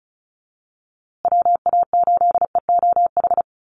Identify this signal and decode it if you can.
Morse code, heard as a keyed tone, about 34 wpm, 700 Hz. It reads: WA8EOH
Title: USA Call Signs Repeated 2x 35wpm